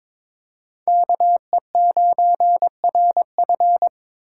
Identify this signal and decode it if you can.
Morse code, heard as a keyed tone, about 21 wpm, 700 Hz. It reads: KE9RF
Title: USA Call Signs - 5 Characters - 3x Character Spacing - 22wpm